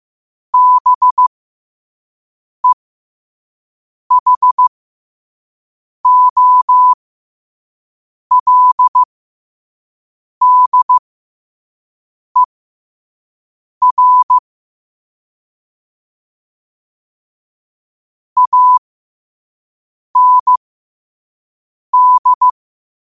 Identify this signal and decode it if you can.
Morse code, heard as a keyed tone, about 15 wpm, spaced as for 5 wpm, 1000 Hz. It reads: BEHOLDER AND
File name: qotdfiles / 5wpm